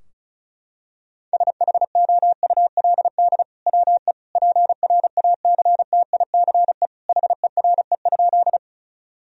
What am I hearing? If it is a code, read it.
Morse code, heard as a keyed tone, about 34 wpm, 700 Hz. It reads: SHOULD WE PRACTICE HERE?
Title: Sentences from Top 1000 Words - Review Part 4 - 35wpm